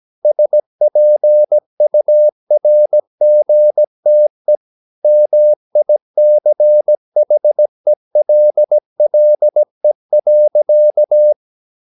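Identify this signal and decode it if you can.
Morse code, heard as a keyed tone, about 17 wpm, 600 Hz. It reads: SPURGTE MICHELLE.
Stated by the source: Slow (QRS) Morse practice at 17 wpm, from CW med Gnister.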